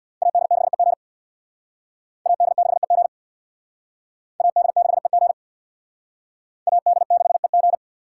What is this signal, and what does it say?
Morse code, heard as a keyed tone, about 64 wpm, 700 Hz. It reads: AD6EG AD6EG AD6EG AD6EG